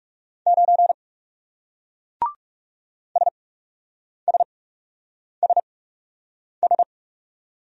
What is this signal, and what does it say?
Morse code, heard as a keyed tone, about 41 wpm, spaced as for 8 wpm, 700 Hz. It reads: 9 SSSS